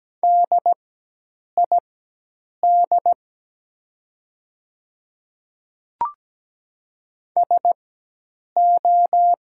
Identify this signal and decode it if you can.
Morse code, heard as a keyed tone, about 17 wpm, spaced as for 8 wpm, 700 Hz. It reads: DID SO
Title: 4x Character Spacing - Top 100 Words - 17wpm